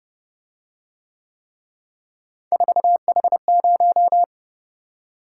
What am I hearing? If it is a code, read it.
Morse code, heard as a keyed tone, about 29 wpm, 700 Hz. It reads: 4H0